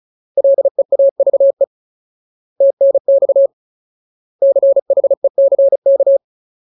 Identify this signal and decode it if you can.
Morse code, heard as a keyed tone, about 34 wpm, 550 Hz. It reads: LEAVE TNX CHECK
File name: Ultimate Word List / 35WPM Ultimate